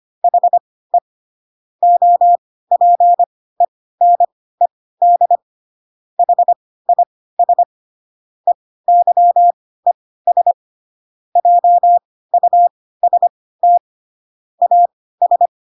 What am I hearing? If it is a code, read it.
Morse code, heard as a keyed tone, about 25 wpm, spaced as for 16 wpm, 700 Hz. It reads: HE OPENED HIS EYES JUST AS
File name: Pinocchio / Pinocchio ch6 10 25 ik1hge